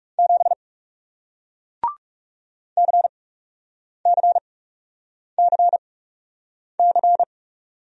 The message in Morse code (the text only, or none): 7 CCCC